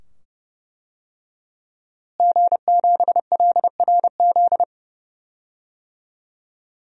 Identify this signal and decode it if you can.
Morse code, heard as a keyed tone, about 32 wpm, 700 Hz. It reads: G7LRZ
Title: DXCC-100 Call Signs - 5 Characters - Repeated 2x - 30wpm